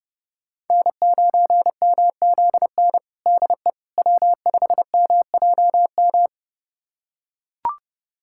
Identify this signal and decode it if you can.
Morse code, heard as a keyed tone, about 32 wpm, 700 Hz. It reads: N9MZN DE W5MJM
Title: USA Callsign DE Callsign - 5 Characters - 30wpm